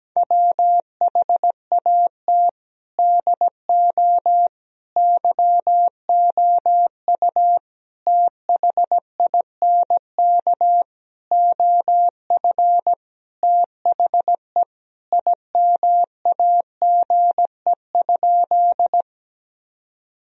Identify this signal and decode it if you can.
Morse code, heard as a keyed tone, about 17 wpm, 700 Hz. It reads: WHAT DO YOU THINK OF THE IMAGE?
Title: Sentences from Top 1000 Words - Review Part 4 - 17wpm